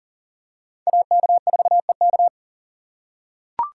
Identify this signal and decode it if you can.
Morse code, heard as a keyed tone, about 40 wpm, 700 Hz. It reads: AK4EK
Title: USA Call Signs - 5 Characters - 4x Character Spacing - 40wpm